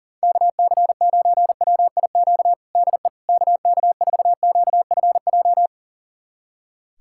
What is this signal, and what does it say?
Morse code, heard as a keyed tone, about 40 wpm, 700 Hz. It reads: KC9WIQ DE KK4QFJ
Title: USA Callsign DE Callsign - 6 Characters - 40wpm